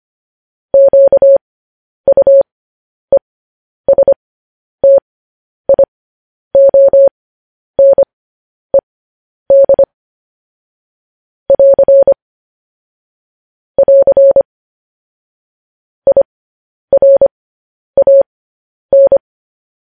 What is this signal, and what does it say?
Morse code, heard as a keyed tone, about 25 wpm, spaced as for 10 wpm, 550 Hz. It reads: QUESTIONED <AR> <AR> IRAN
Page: News Headlines in Morse Code at 10 WPM